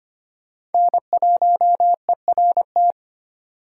Morse code, heard as a keyed tone, about 34 wpm, spaced as for 30 wpm, 700 Hz. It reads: N1ERT
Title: Speed Racing - USA Call Signs 25wpm